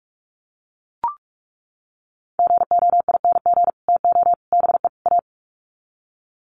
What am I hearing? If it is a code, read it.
Morse code, heard as a keyed tone, about 44 wpm, 700 Hz. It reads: GOING TO BE A
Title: Sets of 4 Words 45wpm